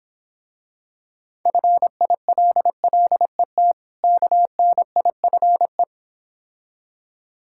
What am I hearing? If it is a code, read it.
Morse code, heard as a keyed tone, about 26 wpm, 700 Hz. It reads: FILLET KNIFE